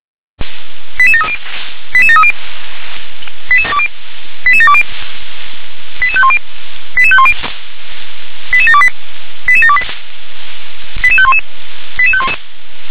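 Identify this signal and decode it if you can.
Five-tone selective call (ZVEI), heard as quick runs of five tones, tones of about 70 ms. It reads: 88730 88730 88720 88510 88510 88728 88728 88511 88511
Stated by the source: ZVEI five-tone selective call received on an SDR.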